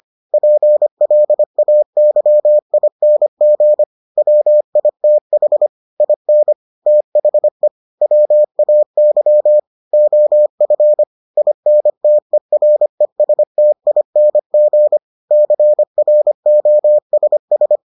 Morse code, heard as a keyed tone, about 25 wpm, 600 Hz. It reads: PLAYING WITH IN THE WAY OF INTERESTING CROSS